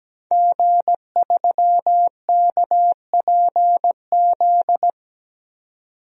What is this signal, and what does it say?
Morse code, heard as a keyed tone, about 17 wpm, 700 Hz. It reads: G3KPZ
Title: DXCC-100 Call Signs - 5 Characters - 17wpm